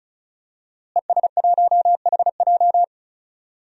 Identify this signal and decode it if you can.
Morse code, heard as a keyed tone, about 36 wpm, 700 Hz. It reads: ES1HJ
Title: DXCC-100 Call Signs - 5 Characters - Repeated 3x - 35wpm